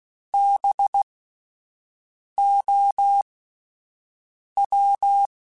Morse code, heard as a keyed tone, about 16 wpm, spaced as for 5 wpm, 800 Hz. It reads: BOW